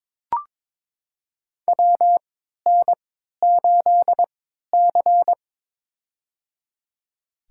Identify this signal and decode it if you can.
Morse code, heard as a keyed tone, about 22 wpm, spaced as for 13 wpm, 700 Hz. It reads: WN8C
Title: USA Call Signs - 4 Characters - 3x Character Spacing - 22wpm